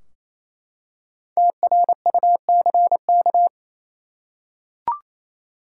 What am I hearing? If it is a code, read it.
Morse code, heard as a keyed tone, about 28 wpm, 700 Hz. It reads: TRUCK
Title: Five Letter Words - 28wpm